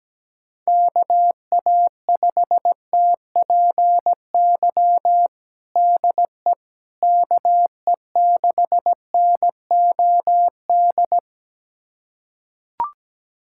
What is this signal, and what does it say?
Morse code, heard as a keyed tone, about 17 wpm, 700 Hz. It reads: KA5TPY DE KE6NOD
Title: USA Callsign DE Callsign - 6 Characters - 17wpm